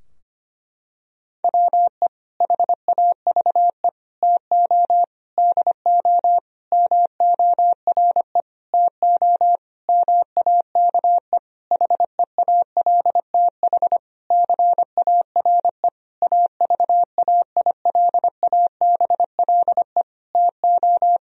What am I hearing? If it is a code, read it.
Morse code, heard as a keyed tone, about 25 wpm, 700 Hz. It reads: WE HAVE TO DO MORE TO MAKE HEALTH CARE AVAILABLE TO
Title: Sentences from Top 1000 Words - Review Part 3 - 25wpm